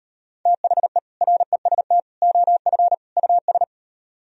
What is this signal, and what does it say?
Morse code, heard as a keyed tone, about 38 wpm, 700 Hz. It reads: THE REST OF US